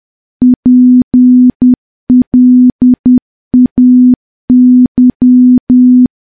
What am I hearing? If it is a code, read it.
Morse code, heard as a keyed tone, about 10 wpm, 250 Hz. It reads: PLAY
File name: morse / morse-training-10WPM